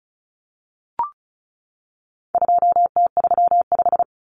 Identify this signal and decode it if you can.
Morse code, heard as a keyed tone, about 34 wpm, 700 Hz. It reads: 2T35
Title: Common Cut Numbers - 4-Digit 35wpm